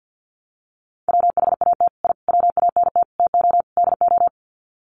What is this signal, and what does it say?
Morse code, heard as a keyed tone, about 52 wpm, 700 Hz. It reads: WHAT I WANT TO DO
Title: Sets of 5 Words Encore 50wpm